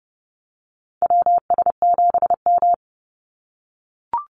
Morse code, heard as a keyed tone, about 30 wpm, 700 Hz. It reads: WS7M
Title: Call Signs - 30wpm